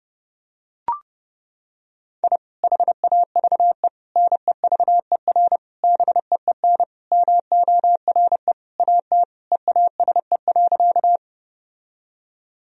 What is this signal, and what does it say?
Morse code, heard as a keyed tone, about 32 wpm, 700 Hz. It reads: I HAVE NEVER BEEN MORE AT EASE.
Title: Sentences from Top 1000 Words - Review Part 1 - 30wpm